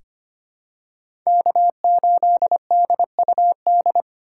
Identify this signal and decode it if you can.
Morse code, heard as a keyed tone, about 25 wpm, 700 Hz. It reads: K8DUD
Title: USA Call Signs - 5 Characters - 2x Character Spacing - 25wpm